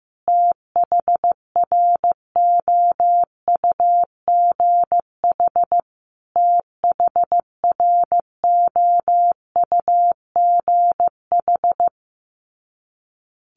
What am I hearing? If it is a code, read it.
Morse code, heard as a keyed tone, about 15 wpm, 700 Hz. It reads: THROUGH THROUGH
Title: Top 300 Words Repeated 15wpm